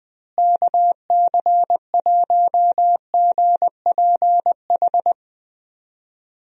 Morse code, heard as a keyed tone, about 20 wpm, 700 Hz. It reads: KC1GPH
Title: USA Call Signs - 6 Characters - Repeated 3x - 20wpm